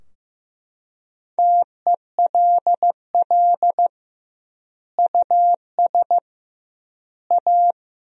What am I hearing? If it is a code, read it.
Morse code, heard as a keyed tone, about 15 wpm, 700 Hz. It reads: TELL US A
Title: 2x Word-Spacing - Sentences from Top 300 Words - 15wpm